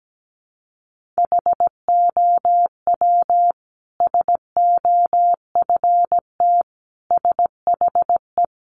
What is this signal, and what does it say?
Morse code, heard as a keyed tone, about 17 wpm, 700 Hz. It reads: HOW SOFT SHE
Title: Sentences from Top 1500 Words 17wpm